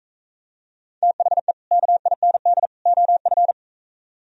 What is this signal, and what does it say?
Morse code, heard as a keyed tone, about 45 wpm, 700 Hz. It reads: THE KIND OF